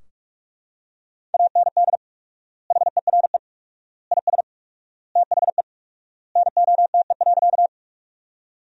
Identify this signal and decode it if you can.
Morse code, heard as a keyed tone, about 41 wpm, 700 Hz. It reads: AND HERE IS THE NOTE.
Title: 4x Word-Spacing - Sentences from Top 500 Words - 45wpm